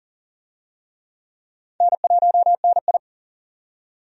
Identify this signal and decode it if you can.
Morse code, heard as a keyed tone, about 40 wpm, 700 Hz. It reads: N1NI
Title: USA Call Signs - 4 Characters - 40wpm